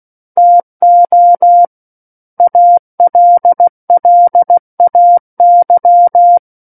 Morse code, heard as a keyed tone, about 16 wpm, 700 Hz. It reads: TO ALLAY
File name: Book One / 07-Chapter 7 - 16 WPM